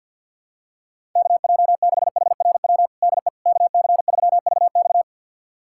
Morse code, heard as a keyed tone, about 46 wpm, 700 Hz. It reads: KJ6HRW DE KC3VX
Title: USA Callsign DE Callsign 50wpm